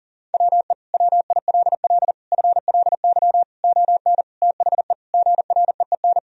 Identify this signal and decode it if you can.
Morse code, heard as a keyed tone, about 37 wpm, 700 Hz. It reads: WE WILL FLY ON THE GREEN
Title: Sentences from Top 1000 Words - Review Part 3 - 40wpm